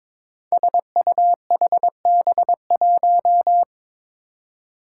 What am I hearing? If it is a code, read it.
Morse code, heard as a keyed tone, about 22 wpm, 700 Hz. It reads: SUHB1
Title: Five Challenging Letter-Number Combinations 22wpm